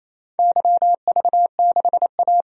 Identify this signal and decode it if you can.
Morse code, heard as a keyed tone, about 28 wpm, 700 Hz. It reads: YV6A
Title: DXCC-100 Call Signs - 4 Characters - 28wpm